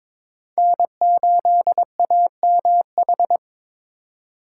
Morse code, heard as a keyed tone, about 22 wpm, 700 Hz. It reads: N8AMH
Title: USA Call Signs - 5 Characters - 4x Character Spacing - 22wpm